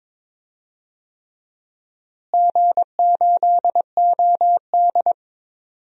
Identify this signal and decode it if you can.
Morse code, heard as a keyed tone, about 22 wpm, 700 Hz. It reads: G8OD
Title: DXCC-100 Call Signs - 4 Characters - Repeated 3x - 22wpm